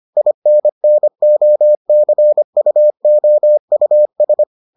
Morse code, heard as a keyed tone, about 25 wpm, 600 Hz. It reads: INNOCUOUS